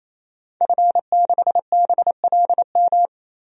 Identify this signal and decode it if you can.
Morse code, heard as a keyed tone, about 28 wpm, 700 Hz. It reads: F6BLM
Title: DXCC-100 Call Signs - 5 Characters - 28wpm